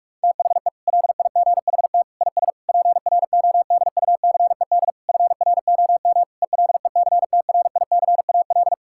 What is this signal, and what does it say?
Morse code, heard as a keyed tone, about 41 wpm, 700 Hz. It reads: THE LIGHT IS PRODUCED FROM ELECTRICAL
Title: Sentences from Top 1000 Words - Review Part 6 - 45wpm